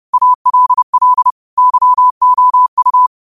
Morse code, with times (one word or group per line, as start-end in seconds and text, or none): 0.1-1.3 ALL
1.6-3.1 YOU